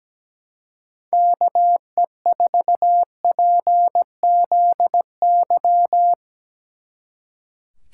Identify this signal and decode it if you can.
Morse code, heard as a keyed tone, about 20 wpm, 700 Hz. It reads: KE4PZY
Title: Speed Racing - USA 2x3 Call Signs 17wpm